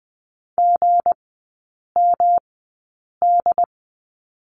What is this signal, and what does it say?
Morse code, heard as a keyed tone, about 20 wpm, spaced as for 8 wpm, 690 Hz. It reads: GMD
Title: 159 Lesson 26e - Letter g - Three Characters 20wpm